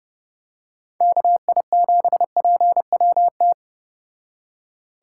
Morse code, heard as a keyed tone, about 30 wpm, 700 Hz. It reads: KI7PWT